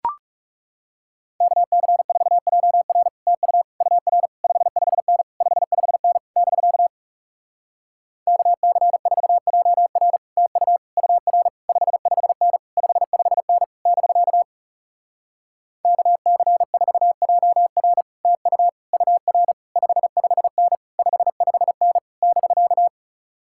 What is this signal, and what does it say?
Morse code, heard as a keyed tone, about 40 wpm, 700 Hz. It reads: KC4JR TU UR 55N 55N <BK> KC4JR TU UR 55N 55N <BK> KC4JR TU UR 55N 55N <BK>